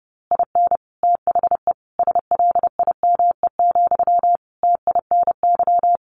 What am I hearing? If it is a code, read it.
Morse code, heard as a keyed tone, about 30 wpm, 700 Hz. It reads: IN THE SLIME, TINY
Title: A Sound Of Thunder 30wpm